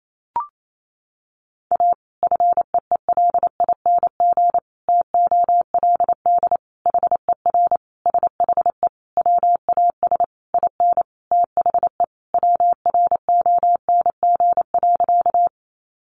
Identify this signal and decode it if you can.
Morse code, heard as a keyed tone, about 28 wpm, 700 Hz. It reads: A FEELING TOLD HER SHE WAS IN THE WRONG.
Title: Sentences from Top 1200 Words 28wpm